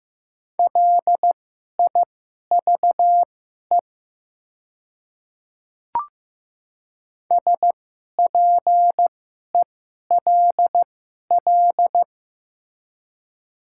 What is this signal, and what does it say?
Morse code, heard as a keyed tone, about 15 wpm, spaced as for 11 wpm, 700 Hz. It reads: LIVE SPELL
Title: 2x Character Spacing - Top 200 Words - 15wpm